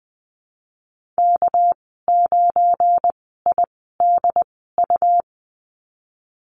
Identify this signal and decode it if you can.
Morse code, heard as a keyed tone, about 20 wpm, spaced as for 15 wpm, 700 Hz. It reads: K9IDU
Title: 219 Lesson 35g - Number 0 - Call Signs 20wpm